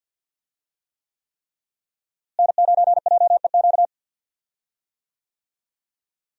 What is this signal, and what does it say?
Morse code, heard as a keyed tone, about 53 wpm, 700 Hz. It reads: N9JEQ
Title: USA Call Signs - 5 Characters - Repeated 3x - 50wpm